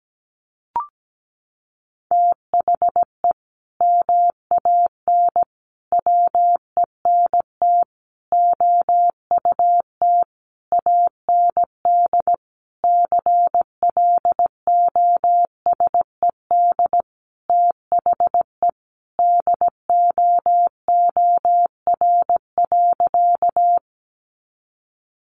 Morse code, heard as a keyed tone, about 17 wpm, 700 Hz. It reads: THE MAN WENT OUT AND CLOSED THE DOOR.
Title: Sentences from Top 1400 Words 17wpm